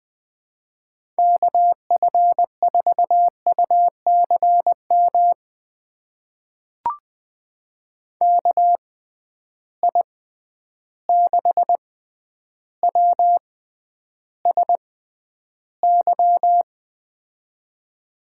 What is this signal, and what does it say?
Morse code, heard as a keyed tone, about 20 wpm, spaced as for 10 wpm, 700 Hz. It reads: KF4UCM KI6WSY
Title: USA Call Signs - 6 Characters - 6x Character Spacing - 20wpm